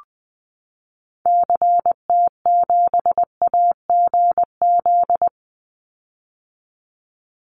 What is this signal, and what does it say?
Morse code, heard as a keyed tone, about 20 wpm, 700 Hz. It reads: CT7AGZ
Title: Call Signs - 20wpm